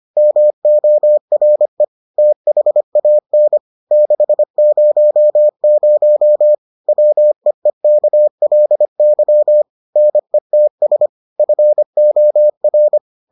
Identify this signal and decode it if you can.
Morse code, heard as a keyed tone, about 25 wpm, 600 Hz. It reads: MORE THAN 600 WEEKLY NETS FOR